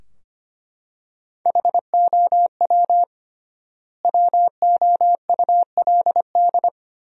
Morse code, heard as a keyed tone, about 25 wpm, 700 Hz. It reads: HOW WOULD
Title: 3x Word-Spacing - Sentences from Top 100 Words - 25wpm